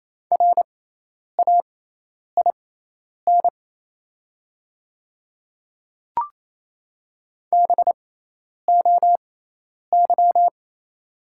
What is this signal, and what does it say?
Morse code, heard as a keyed tone, about 28 wpm, spaced as for 10 wpm, 700 Hz. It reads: RAIN BOY
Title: 6x Character Spacing - Top 500 Words - 28wpm